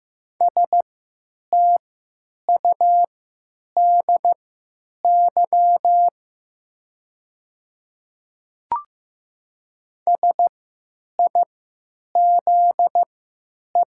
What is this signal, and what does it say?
Morse code, heard as a keyed tone, about 15 wpm, spaced as for 9 wpm, 700 Hz. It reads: STUDY SIZE
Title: 3x Character Spacing - Top 500 Words - 15wpm